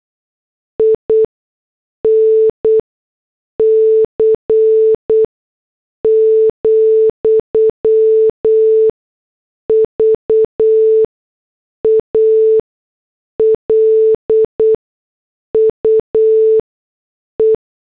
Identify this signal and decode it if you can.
Morse code, heard as a keyed tone, about 8 wpm, spaced as for 6 wpm, 440 Hz. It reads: INC, VALUE